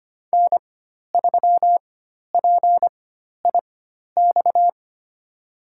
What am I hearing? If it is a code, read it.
Morse code, heard as a keyed tone, about 25 wpm, spaced as for 12 wpm, 700 Hz. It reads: N3PIX
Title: USA Call Signs - 5 Characters - 4x Character Spacing - 25wpm